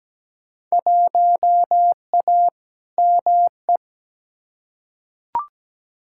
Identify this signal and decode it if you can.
Morse code, heard as a keyed tone, about 17 wpm, 700 Hz. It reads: 1A ME
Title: Field Day Exchanges Encore - 17wpm